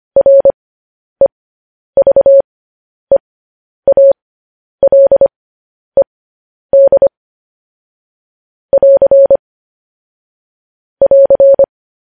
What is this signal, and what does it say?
Morse code, heard as a keyed tone, about 25 wpm, spaced as for 10 wpm, 550 Hz. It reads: REVEALED <AR> <AR>
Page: News Headlines in Morse Code at 10 WPM